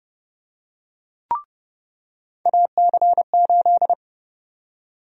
Morse code, heard as a keyed tone, about 30 wpm, 700 Hz. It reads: AC8
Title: USA Call Sign Prefixes - 30wpm